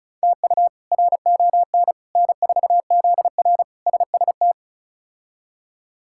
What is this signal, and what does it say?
Morse code, heard as a keyed tone, about 34 wpm, 700 Hz. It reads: TU RON N4ZR SST
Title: K1USN SST QSO Practice - 35wpm